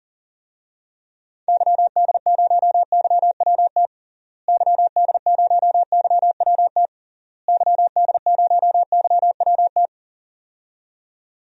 Morse code, heard as a keyed tone, about 40 wpm, 700 Hz. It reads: YD0YWT YD0YWT YD0YWT
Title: DXCC-100 Call Signs - 6 Characters - Repeated 3x - 40wpm